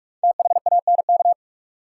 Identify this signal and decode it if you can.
Morse code, heard as a keyed tone, about 43 wpm, 700 Hz. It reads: THANK
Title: Sets of 2 Words - New Word Drill - 45wpm